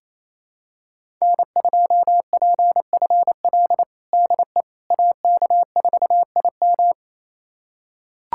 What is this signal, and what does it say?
Morse code, heard as a keyed tone, about 28 wpm, 700 Hz. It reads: N2PFL DE AK4IM
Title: USA Callsign DE Callsign - 5 Characters - 28wpm